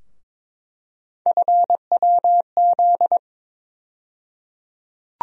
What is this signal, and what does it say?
Morse code, heard as a keyed tone, about 22 wpm, 700 Hz. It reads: FWZ
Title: USA Call Sign Postfixes - 3 Letters 22wpm